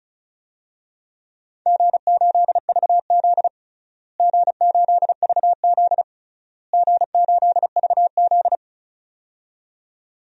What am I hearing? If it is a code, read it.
Morse code, heard as a keyed tone, about 36 wpm, 700 Hz. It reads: G8VZ G8VZ G8VZ
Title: DXCC-100 Call Signs - 4 Characters - Repeated 3x - 35wpm